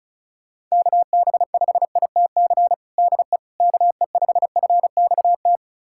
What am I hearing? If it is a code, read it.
Morse code, heard as a keyed tone, about 36 wpm, 700 Hz. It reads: KB5ITC DE KE5FXT
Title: USA Callsign DE Callsign - 6 Characters - 35wpm